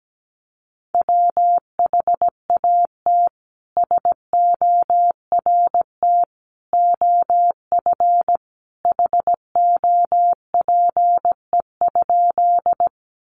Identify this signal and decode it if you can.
Morse code, heard as a keyed tone, about 17 wpm, 700 Hz. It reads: WHAT SORT OF HOPE?
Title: Sentences from Top 1300 Words 17wpm